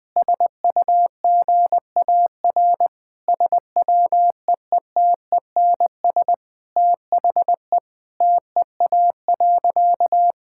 Morse code, heard as a keyed tone, about 20 wpm, 700 Hz. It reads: SUGAR SWEETENS THE TEA.
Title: Sentences with 2-Word Phrases - Spoken Beginning - 20wpm